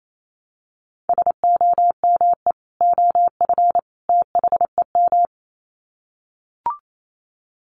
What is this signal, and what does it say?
Morse code, heard as a keyed tone, about 28 wpm, 700 Hz. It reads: SOME OF THEM
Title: Sets of 3 Words 28wpm